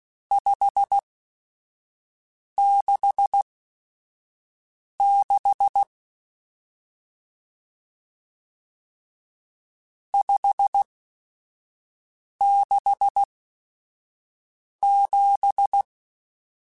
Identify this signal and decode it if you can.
Morse code, heard as a keyed tone, about 16 wpm, spaced as for 5 wpm, 800 Hz. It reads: 566 567
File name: RSTs / RST-mp3s / Various RST @5